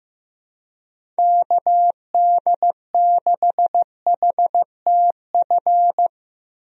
Morse code, heard as a keyed tone, about 15 wpm, 700 Hz. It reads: KD6HTF
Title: USA Call Signs - 6 Characters - Repeated 2x - 15wpm